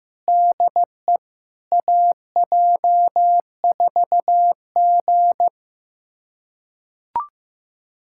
Morse code, heard as a keyed tone, about 16 wpm, 700 Hz. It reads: DE AJ4G
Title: USA Callsign DE Callsign - 4 Characters - 15wpm